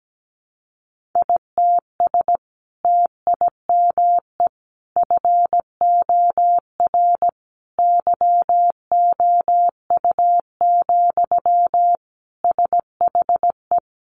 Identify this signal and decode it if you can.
Morse code, heard as a keyed tone, about 17 wpm, 700 Hz. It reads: ITS TIME FOR YOU, SHE
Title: Sentences from Top 600 Words 17wpm